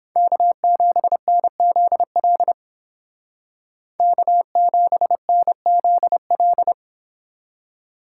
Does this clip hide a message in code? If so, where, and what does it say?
Yes: Morse code from 0.2 to 6.7 s, heard as a keyed tone, about 31 wpm, 700 Hz. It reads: K7NZL K7NZL